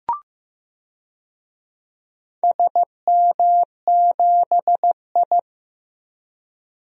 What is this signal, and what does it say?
Morse code, heard as a keyed tone, about 15 wpm, 700 Hz. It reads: SM7I
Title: DXCC-100 Call Signs - 4 Characters - 15wpm